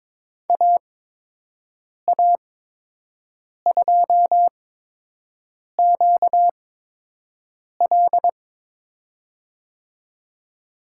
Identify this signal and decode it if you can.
Morse code, heard as a keyed tone, about 21 wpm, spaced as for 6 wpm, 700 Hz. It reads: AA2QL
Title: USA Call Signs - 5 Characters - 8x Character Spacing - 22wpm